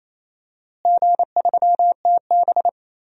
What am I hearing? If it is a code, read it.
Morse code, heard as a keyed tone, about 28 wpm, 700 Hz. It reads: G3TB
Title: DXCC-100 Call Signs - 4 Characters - Repeated 2x - 28wpm